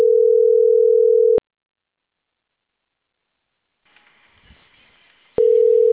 Ringback tone (North America), 440 + 480 Hz.